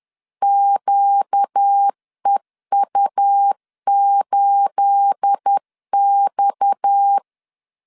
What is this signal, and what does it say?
Morse code, heard as a keyed tone, about 10 wpm, 800 Hz. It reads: QEU8X
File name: CW / DET / 1 9 1500C S B 10wpm 4439